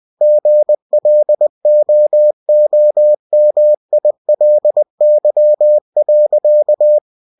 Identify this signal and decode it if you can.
Morse code, heard as a keyed tone, about 20 wpm, 600 Hz. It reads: GLOOMILY.
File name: cw-books / winnie-the-pooh / winnie-20wpm-page-0045